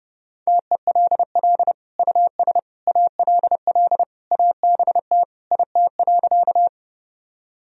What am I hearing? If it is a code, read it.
Morse code, heard as a keyed tone, about 30 wpm, 700 Hz. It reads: TELL US ALL ABT IT.
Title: Sentences from Top 200 Words and Abbreviations 30wpm